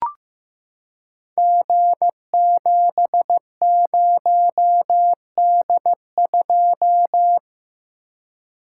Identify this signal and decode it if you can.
Morse code, heard as a keyed tone, about 15 wpm, 700 Hz. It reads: G70D2
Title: Code Groups - 5 Letter:Numbers - 15wpm